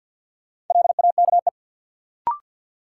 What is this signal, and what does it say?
Morse code, heard as a keyed tone, about 53 wpm, 700 Hz. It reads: LAKE
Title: Sentences from Top 1000 Words - Review Part 3 - New Word Drill - 50wpm